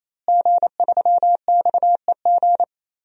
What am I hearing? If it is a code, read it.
Morse code, heard as a keyed tone, about 28 wpm, 700 Hz. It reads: G3XEG
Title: DXCC-100 Call Signs - 5 Characters - Repeated 3x - 28wpm